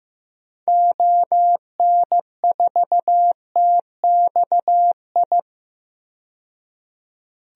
Morse code, heard as a keyed tone, about 15 wpm, 700 Hz. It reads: ON4TXI
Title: DXCC-100 Call Signs - 6 Characters - 15wpm